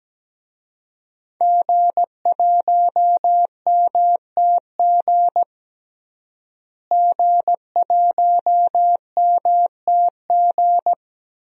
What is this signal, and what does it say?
Morse code, heard as a keyed tone, about 17 wpm, 700 Hz. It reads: G1MTG G1MTG
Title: DXCC-100 Call Signs - 5 Characters - Repeated 2x - 17wpm